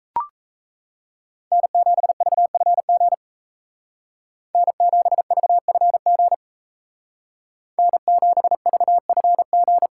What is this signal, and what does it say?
Morse code, heard as a keyed tone, about 34 wpm, 700 Hz. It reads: N7VFG N7VFG N7VFG
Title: Speed Racing - USA Call Signs 28wpm